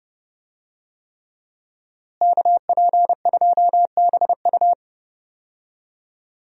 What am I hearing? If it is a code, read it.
Morse code, heard as a keyed tone, about 30 wpm, 700 Hz. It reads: KP2BU
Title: USA Call Signs - 5 Characters - Repeated 3x - 30wpm